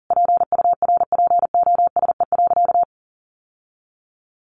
Morse code, heard as a keyed tone, about 40 wpm, 700 Hz. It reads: PURPOSE.
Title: Sentences from Top 800 Words 40wpm